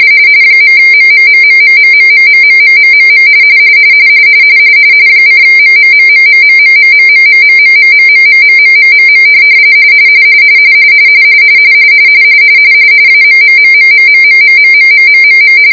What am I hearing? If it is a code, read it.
RTTY, about 45.45 baud. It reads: .MMMMMMMMMMMMMM :MMMMMMMMMMMMMMMMMMMMMM AMMMMMMMMMMMMMM